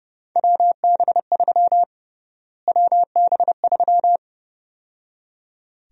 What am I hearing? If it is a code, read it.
Morse code, heard as a keyed tone, about 31 wpm, 700 Hz. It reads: WB3 WB3
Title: USA Call Sign Prefixes - Repeated 2x - 30wpm